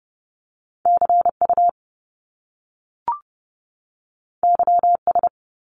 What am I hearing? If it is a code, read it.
Morse code, heard as a keyed tone, about 30 wpm, 700 Hz. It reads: CU YS
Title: Top Two Letter Combinations in Words 30wpm